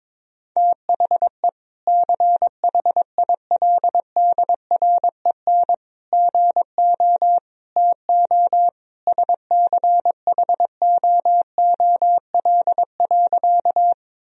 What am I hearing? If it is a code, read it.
Morse code, heard as a keyed tone, about 21 wpm, 700 Hz. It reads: THE CHILDREN GO TO SCHOOL.
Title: Sentences from Top 1000 Words - Review Part 1 - 22wpm